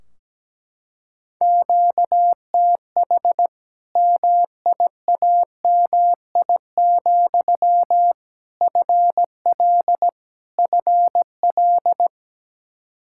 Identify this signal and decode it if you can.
Morse code, heard as a keyed tone, about 17 wpm, 700 Hz. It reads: QTH MIAMI, FL FL
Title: Practice QSO - Exchange 1 - Beginning - 17wpm